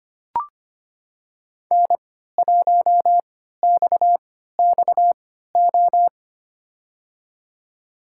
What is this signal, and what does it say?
Morse code, heard as a keyed tone, about 25 wpm, spaced as for 14 wpm, 700 Hz. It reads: N1XXO
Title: USA Call Signs - 5 Characters - 3x Character Spacing - 25wpm